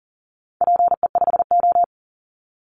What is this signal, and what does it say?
Morse code, heard as a keyed tone, about 40 wpm, 700 Hz. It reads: PE5O
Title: Call Signs - 40wpm